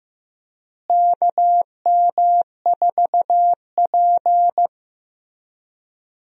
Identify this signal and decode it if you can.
Morse code, heard as a keyed tone, about 15 wpm, 700 Hz. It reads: KM4P
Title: USA Call Signs - 4 Characters - Repeated 3x - 15wpm